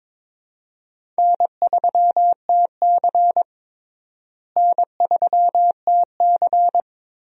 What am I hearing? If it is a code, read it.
Morse code, heard as a keyed tone, about 22 wpm, 700 Hz. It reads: N3TC N3TC
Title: USA Call Signs - 4 Characters - Repeated 3x - 22wpm